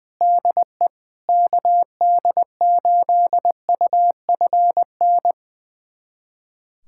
Morse code, heard as a keyed tone, about 20 wpm, 700 Hz. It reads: DE KD8UFN
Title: USA Callsign DE Callsign - 6 Characters - 20wpm